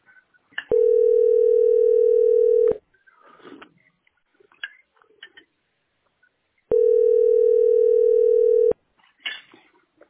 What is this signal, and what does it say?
Ringback tone (North America), 440 + 480 Hz.